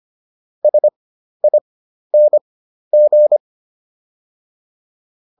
Morse code, heard as a keyed tone, about 25 wpm, spaced as for 12 wpm, 600 Hz. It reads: SING